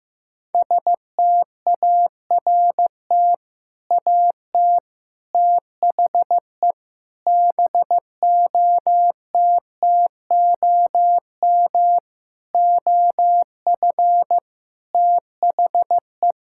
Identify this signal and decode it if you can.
Morse code, heard as a keyed tone, about 16 wpm, 700 Hz. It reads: START AT THE BOTTOM OF THE
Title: Sentences from Top 1000 Words - Review Part 6 - 15wpm